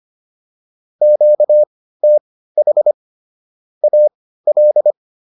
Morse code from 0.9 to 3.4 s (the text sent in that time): QTH